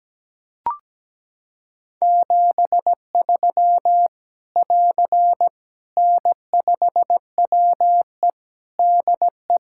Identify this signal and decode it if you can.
Morse code, heard as a keyed tone, about 17 wpm, 700 Hz. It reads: 73 <AR> N5WE DE
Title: K1USN SST QSO Practice 17wpm